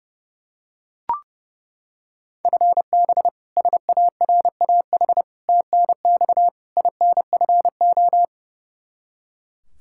Morse code, heard as a keyed tone, about 30 wpm, 700 Hz. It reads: FB SARAH TNX INFO
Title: K1USN SST QSO Practice 30wpm